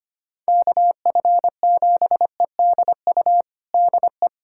Morse code, heard as a keyed tone, about 25 wpm, 700 Hz. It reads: KF7EDU DE
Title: USA Callsign DE Callsign - 6 Characters - 25wpm